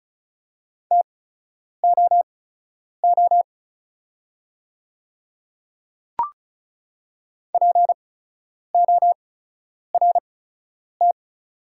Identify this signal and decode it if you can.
Morse code, heard as a keyed tone, about 36 wpm, spaced as for 10 wpm, 690 Hz. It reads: TOO PORT
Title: 8x Character Spacing - Top 200 Words - 35wpm